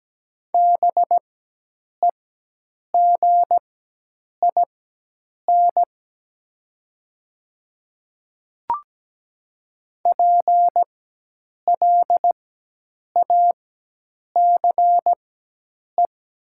Morse code, heard as a keyed tone, about 17 wpm, spaced as for 8 wpm, 700 Hz. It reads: BEGIN PLACE
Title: 4x Character Spacing - Top 500 Words - 17wpm